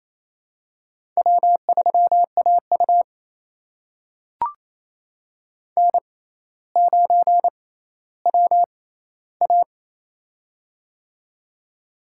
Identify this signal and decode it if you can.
Morse code, heard as a keyed tone, about 28 wpm, spaced as for 14 wpm, 700 Hz. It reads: W3AU N9WA
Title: USA Call Signs - 4 Characters - 6x Character Spacing - 28wpm